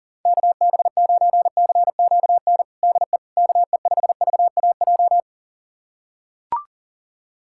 Morse code, heard as a keyed tone, about 37 wpm, 700 Hz. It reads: KB9CQN DE KE5VAJ
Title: USA Callsign DE Callsign - 6 Characters - 40wpm